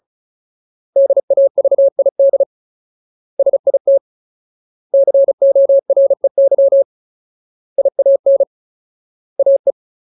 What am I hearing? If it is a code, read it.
Morse code, heard as a keyed tone, about 34 wpm, 550 Hz. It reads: DAVID SIT COREY IAN AE
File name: Ultimate Word List / 35WPM Ultimate